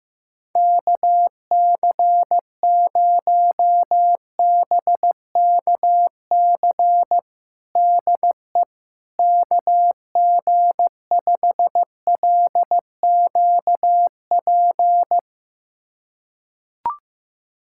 Morse code, heard as a keyed tone, about 16 wpm, 700 Hz. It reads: KC0BKC DE KG5LQP
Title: USA Callsign DE Callsign - 6 Characters - 15wpm